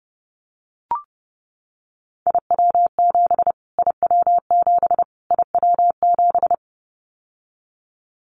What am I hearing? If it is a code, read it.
Morse code, heard as a keyed tone, about 30 wpm, 700 Hz. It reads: IW7 IW7 IW7
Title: Callsign Prefixes for Top 25 DXCC - Repeated 3x - 30wpm